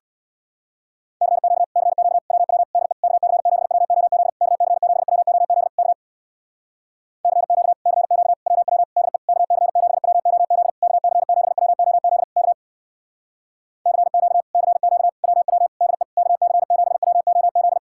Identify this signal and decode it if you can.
Morse code, heard as a keyed tone, about 66 wpm, 700 Hz. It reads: CQ CQ WW DE KC7WOZ KC7WOZ K CQ CQ WW DE KC7WOZ KC7WOZ K CQ CQ WW DE KC7WOZ